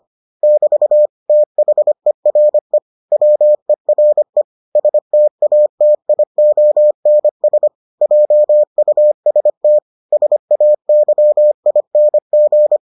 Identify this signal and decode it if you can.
Morse code, heard as a keyed tone, about 25 wpm, 600 Hz. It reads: <BT> THERE WERE STATIONS JUST SAYING